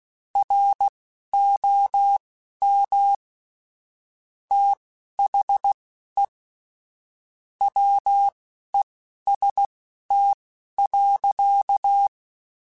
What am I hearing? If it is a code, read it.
Morse code, heard as a keyed tone, about 16 wpm, spaced as for 11 wpm, 800 Hz. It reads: ROM THE WEST.
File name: Weather Sets / Weather Sets-mp3s / Weather 5 1 @10